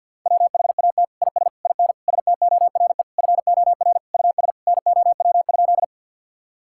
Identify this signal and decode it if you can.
Morse code, heard as a keyed tone, about 50 wpm, 700 Hz. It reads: WHAT IS IN STORE FOR US NOW?
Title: Sentences from Top 1000 Words - Review Part 5 - 50wpm